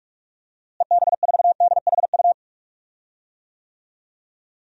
Morse code, heard as a keyed tone, about 46 wpm, 690 Hz. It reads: EB4DHU
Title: DXCC-100 Call Signs - 6 Characters - Repeated 2x - 45wpm